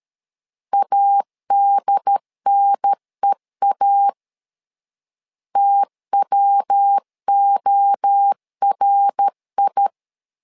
Morse code, heard as a keyed tone, about 13 wpm, 800 Hz. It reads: ADNEA TWORI
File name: LEZIONI / 3 MWRO / 3 MWRO 12wpm 1730